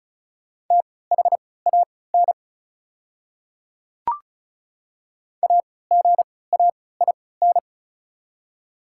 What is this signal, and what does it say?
Morse code, heard as a keyed tone, about 34 wpm, spaced as for 20 wpm, 700 Hz. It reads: THAN AGAIN
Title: 3x Character Spacing - Top 200 Words - 35wpm